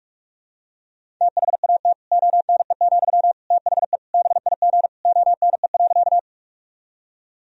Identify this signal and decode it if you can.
Morse code, heard as a keyed tone, about 46 wpm, 700 Hz. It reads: THAT ONE, THE BIG ONE.
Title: Sentences from Top 1000 Words - Review Part 1 - 45wpm